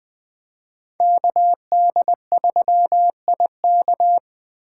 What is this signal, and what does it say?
Morse code, heard as a keyed tone, about 24 wpm, 700 Hz. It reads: KD3IK
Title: Speed Racing - USA 2x2 Call Signs 20wpm